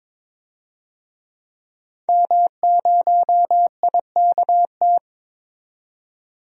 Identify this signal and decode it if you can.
Morse code, heard as a keyed tone, about 21 wpm, 700 Hz. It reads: M0IKT
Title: DXCC-100 Call Signs - 5 Characters - 22wpm